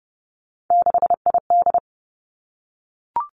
Morse code, heard as a keyed tone, about 30 wpm, 700 Hz. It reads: 6ID
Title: Three Challenging Letter-Number Combinations 30wpm